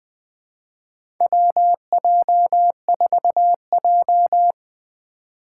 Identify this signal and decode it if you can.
Morse code, heard as a keyed tone, about 20 wpm, 700 Hz. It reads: WJ4J
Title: USA Call Signs - 4x Character Spacing - 20wpm